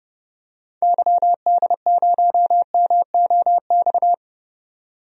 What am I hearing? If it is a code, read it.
Morse code, heard as a keyed tone, about 30 wpm, 700 Hz. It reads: YD0MOX
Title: DXCC-100 Call Signs - 6 Characters - 30wpm